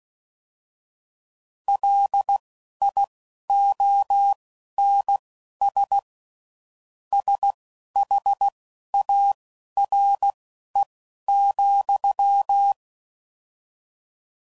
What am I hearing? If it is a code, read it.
Morse code, heard as a keyed tone, about 16 wpm, spaced as for 11 wpm, 800 Hz. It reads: LIONS SHARE,
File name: Mixed / Misc / WordPairs